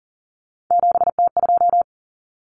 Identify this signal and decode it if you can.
Morse code, heard as a keyed tone, about 40 wpm, 700 Hz. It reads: 7T2
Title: Common Cut Numbers - 3-Digit 40wpm